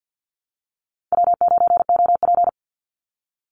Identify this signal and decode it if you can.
Morse code, heard as a keyed tone, about 50 wpm, 700 Hz. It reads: W9OP
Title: Call Signs Repeated 2x - 50wpm